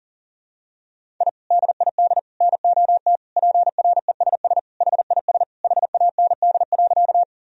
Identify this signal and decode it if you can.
Morse code, heard as a keyed tone, about 40 wpm, 700 Hz. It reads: I DID NOT PRESS HIS HAND.
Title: Sentences from Top 1000 Words - Review Part 1 - 40wpm